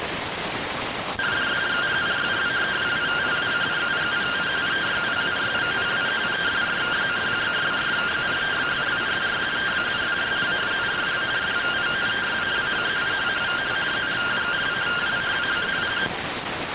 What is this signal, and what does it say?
Two-tone teleprinter signal, about 170 Hz shift.